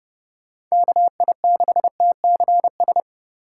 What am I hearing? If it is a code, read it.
Morse code, heard as a keyed tone, about 37 wpm, 700 Hz. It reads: KI6TCS
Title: Speed Racing - USA 2x3 Call Signs 30wpm